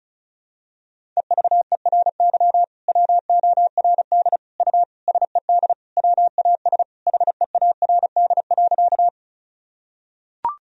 Morse code, heard as a keyed tone, about 36 wpm, 700 Hz. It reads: EVERY WORD U SED WAS HEARD.
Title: Sentences from Top 400 Words and Abbreviations 35wpm